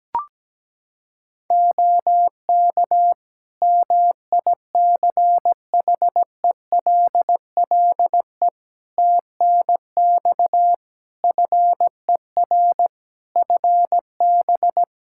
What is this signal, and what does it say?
Morse code, heard as a keyed tone, about 17 wpm, 700 Hz. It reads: OK MICHELLE TNX FER FB
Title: Practice QSO - Exchange 4 - Ending - 17wpm